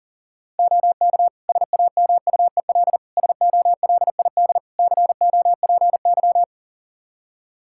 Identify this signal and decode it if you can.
Morse code, heard as a keyed tone, about 40 wpm, 700 Hz. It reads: OK SAMUEL SOLID COPY
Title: K1USN SST QSO Practice 40wpm